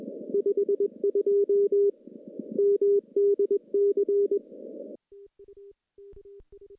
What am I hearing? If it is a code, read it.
Morse code, heard as a keyed tone, about 21 wpm, 400 Hz. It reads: 52 MDC
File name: contests / CW-SS-k5tr- -1400